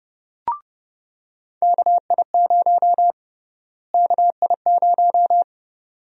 Morse code, heard as a keyed tone, about 30 wpm, 700 Hz. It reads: KI0 KI0
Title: USA Call Sign Prefixes - Repeated 2x - 30wpm